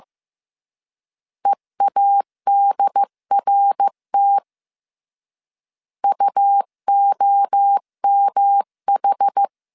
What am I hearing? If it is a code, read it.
Morse code, heard as a keyed tone, about 15 wpm, 800 Hz. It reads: EADRT UOMH
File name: CW / DET / 1 3 1500C S C 14wpm 2458